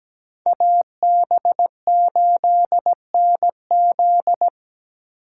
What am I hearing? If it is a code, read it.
Morse code, heard as a keyed tone, about 17 wpm, 700 Hz. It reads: AB8NZ
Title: USA Call Signs - 5 Characters - 6x Character Spacing - 17wpm